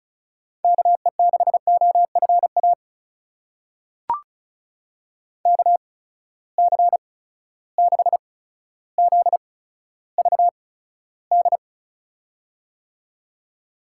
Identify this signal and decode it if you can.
Morse code, heard as a keyed tone, about 36 wpm, spaced as for 15 wpm, 700 Hz. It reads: KE6OFA KC6ZVD
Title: USA Call Signs - 6 Characters - 8x Character Spacing - 35wpm